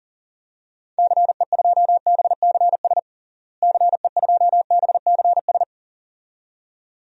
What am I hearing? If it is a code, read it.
Morse code, heard as a keyed tone, about 40 wpm, 700 Hz. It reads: CE2BCS CE2BCS
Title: DXCC-100 Call Signs - 6 Characters - Repeated 2x - 40wpm